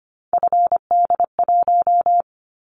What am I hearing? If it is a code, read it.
Morse code, heard as a keyed tone, about 25 wpm, 700 Hz. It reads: FD1
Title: Callsign Prefixes for Top 50 DXCC - 25wpm